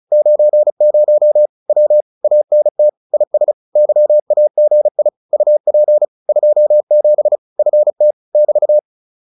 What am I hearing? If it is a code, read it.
Morse code, heard as a keyed tone, about 34 wpm, 600 Hz. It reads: 90 W ANT IS YAGI UP 27 FT <BT>